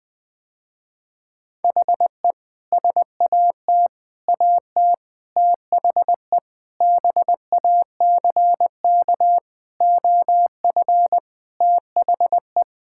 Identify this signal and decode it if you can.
Morse code, heard as a keyed tone, about 20 wpm, 700 Hz. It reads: HE SAT AT THE BACK OF THE
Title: Sentences with 5-Word Phrases - Spoken Beginning - 20wpm